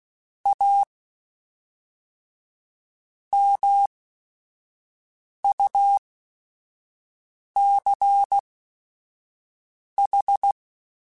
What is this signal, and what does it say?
Morse code, heard as a keyed tone, about 16 wpm, spaced as for 5 wpm, 800 Hz. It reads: A MUCH
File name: Technical Subjects